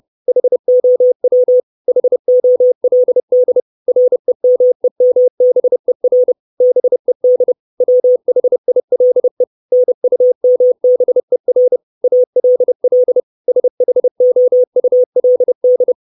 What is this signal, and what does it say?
Morse code, heard as a keyed tone, about 30 wpm, 500 Hz. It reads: HOW HOLD REMEMBER BED WHILE NUMBER ALL SHOULD
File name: 632 Words 30WPM